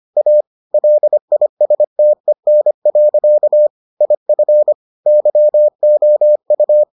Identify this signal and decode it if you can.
Morse code, heard as a keyed tone, about 25 wpm, 600 Hz. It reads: A LISTEN. IF YOU